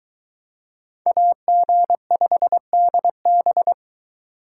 Morse code, heard as a keyed tone, about 23 wpm, 700 Hz. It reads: AG5DB